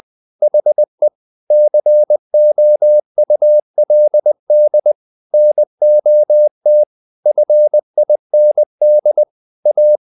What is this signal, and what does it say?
Morse code, heard as a keyed tone, about 20 wpm, 600 Hz. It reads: HE COULD NOT FIND A
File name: the old man and the sea part 2 20wpm